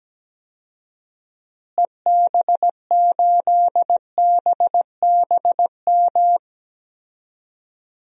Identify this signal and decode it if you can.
Morse code, heard as a keyed tone, about 17 wpm, 700 Hz. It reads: EB8BBM
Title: DXCC-100 Call Signs - 6 Characters - Repeated 2x - 17wpm